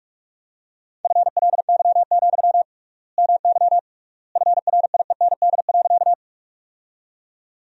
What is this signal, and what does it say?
Morse code, heard as a keyed tone, about 47 wpm, 700 Hz. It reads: FLY, MY FRIEND.
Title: 3x Word-Spacing - Sentences from Top 500 Words - 45wpm